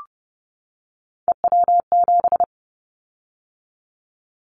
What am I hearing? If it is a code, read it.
Morse code, heard as a keyed tone, about 30 wpm, 700 Hz. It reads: EW7
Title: Callsign Prefixes for Top 50 DXCC - 30wpm